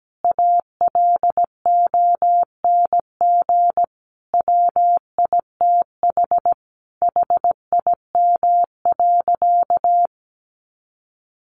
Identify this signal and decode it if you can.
Morse code, heard as a keyed tone, about 17 wpm, 700 Hz. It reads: ALONG WITH HIM.
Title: Sentences from Top 700 Words 17wpm